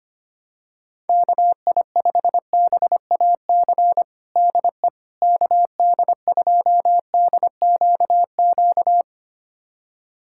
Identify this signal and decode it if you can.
Morse code, heard as a keyed tone, about 25 wpm, 700 Hz. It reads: KI5BAC DE KD2DQQ
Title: USA Callsign DE Callsign - 6 Characters - 25wpm